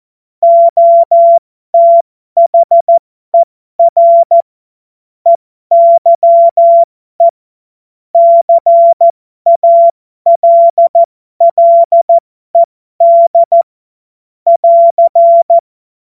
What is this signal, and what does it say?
Morse code, heard as a keyed tone, about 14 wpm, spaced as for 12 wpm, 680 Hz. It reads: OTHER EYE CALLED <AR>
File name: Morsekurs / Witze